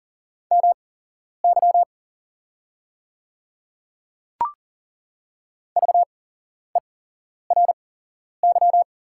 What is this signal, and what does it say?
Morse code, heard as a keyed tone, about 38 wpm, spaced as for 11 wpm, 690 Hz. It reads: MY VERY
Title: 8x Character Spacing - Top 200 Words - 40wpm